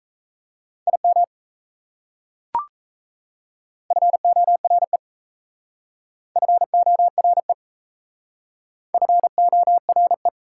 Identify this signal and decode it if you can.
Morse code, heard as a keyed tone, about 37 wpm, 700 Hz. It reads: IM FORE FORE FORE